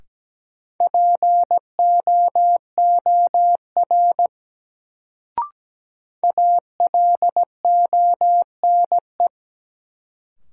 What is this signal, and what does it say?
Morse code, heard as a keyed tone, about 17 wpm, 700 Hz. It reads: POOR ALONE
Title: Sentences from Top 1000 Words - New Word Drill - 17wpm